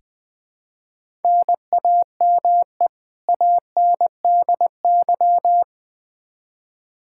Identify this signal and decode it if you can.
Morse code, heard as a keyed tone, about 20 wpm, 700 Hz. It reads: NAME ANDY
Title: Practice QSO - Exchange 1 - Beginning - 20wpm